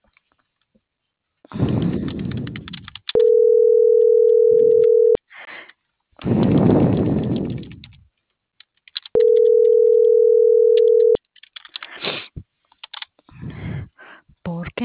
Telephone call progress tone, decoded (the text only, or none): ringback tone